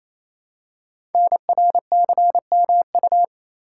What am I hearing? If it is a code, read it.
Morse code, heard as a keyed tone, about 29 wpm, 700 Hz. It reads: NRCMU
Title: Code Groups - 5 Letters - 28wpm